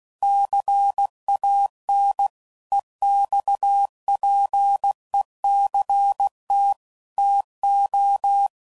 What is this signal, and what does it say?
Morse code, heard as a keyed tone, about 16 wpm, 800 Hz. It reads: CAN EXPECT TO
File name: CW Code Groups @15WPM / Technical Subjects / PwrSupply